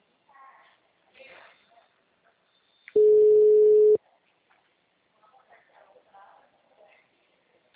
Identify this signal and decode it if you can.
Ringback tone (Europe, 425 Hz), with speech.